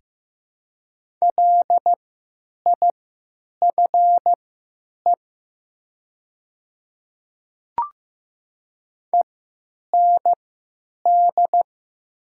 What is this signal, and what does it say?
Morse code, heard as a keyed tone, about 15 wpm, spaced as for 9 wpm, 700 Hz. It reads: LIFE END
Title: 3x Character Spacing - Top 300 Words - 15wpm